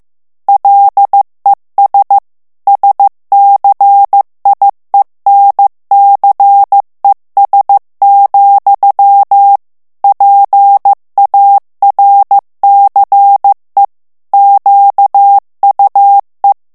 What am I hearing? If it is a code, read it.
Morse code, heard as a keyed tone, about 15 wpm, 800 Hz. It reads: LES SCIENCES, PARCE QUE
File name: morse mp3 / FRANCAIS TEXTE CLAIR 1 VITESSE 14 MOTS MINUTES